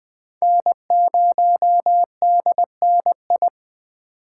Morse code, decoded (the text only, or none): N0DNI